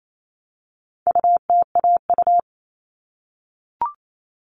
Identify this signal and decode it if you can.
Morse code, heard as a keyed tone, about 28 wpm, 700 Hz. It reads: UTAU
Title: All Cut Numbers - 4-Digit 28wpm